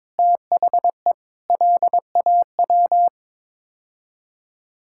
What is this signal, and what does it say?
Morse code, heard as a keyed tone, about 22 wpm, 700 Hz. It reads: THE LAW
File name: practice / 22wpm / Sets of 7 Words - 22wpm